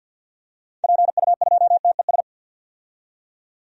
Morse code, heard as a keyed tone, about 51 wpm, 700 Hz. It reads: PU2TES
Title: DXCC-100 Call Signs - 6 Characters - 50wpm